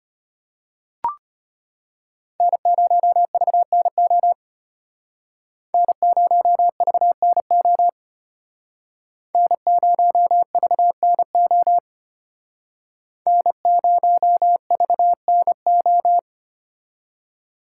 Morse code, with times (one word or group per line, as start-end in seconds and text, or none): 2.4-4.3 N0VNO
5.7-7.9 N0VNO
9.3-11.8 N0VNO
13.3-16.2 N0VNO